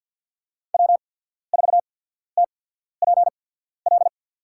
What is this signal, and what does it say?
Morse code, heard as a keyed tone, about 54 wpm, spaced as for 14 wpm, 700 Hz. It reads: W4TPL
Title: USA Call Signs - 5 Characters - 8x Character Spacing - 50wpm